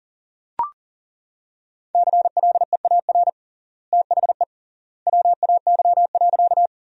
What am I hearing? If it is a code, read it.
Morse code, heard as a keyed tone, about 40 wpm, 700 Hz. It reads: CLEAR THE WAY.
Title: 3x Word-Spacing - Sentences from Top 500 Words - 40wpm